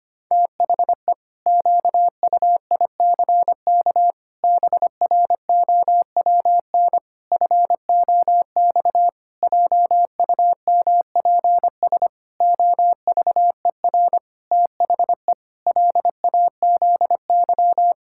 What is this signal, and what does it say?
Morse code, heard as a keyed tone, about 25 wpm, 700 Hz. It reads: THE QUICK BROWN FOX JUMPS OVER THE LAZY
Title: Pangram Challenge v2 - 25wpm